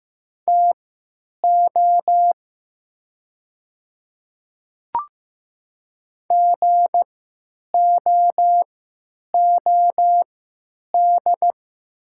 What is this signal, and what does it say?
Morse code, heard as a keyed tone, about 15 wpm, spaced as for 9 wpm, 700 Hz. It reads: TO GOOD
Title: 3x Character Spacing - Top 200 Words - 15wpm